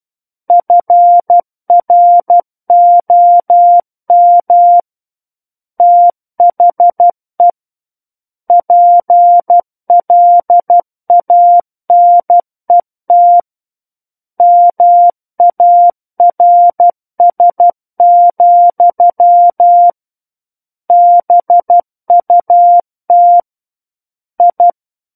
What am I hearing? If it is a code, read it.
Morse code, heard as a keyed tone, about 12 wpm, 700 Hz. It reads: FROM THE PLANET MARS, BUT I
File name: Book One / 03-Chapter 3 - 12 WPM